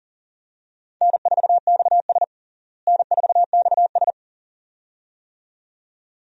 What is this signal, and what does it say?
Morse code, heard as a keyed tone, about 37 wpm, 700 Hz. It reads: N4XS N4XS
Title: USA Call Signs - 4 Characters - Repeated 2x - 40wpm